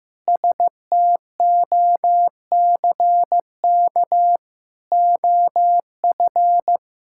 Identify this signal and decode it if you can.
Morse code, heard as a keyed tone, about 16 wpm, 700 Hz. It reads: STOCK OF
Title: Sentences from Top 1000 Words - Review Part 5 - 15wpm